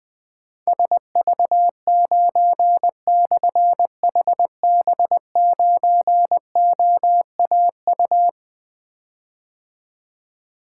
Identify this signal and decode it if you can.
Morse code, heard as a keyed tone, about 20 wpm, 700 Hz. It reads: SV9/HB9OAU
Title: Difficult Call Signs 20wpm